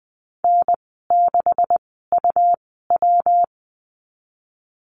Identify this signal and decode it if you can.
Morse code, heard as a keyed tone, about 20 wpm, spaced as for 15 wpm, 700 Hz. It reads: N6UW
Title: 179 Lesson 29g - Slash - Call Signs 20wpm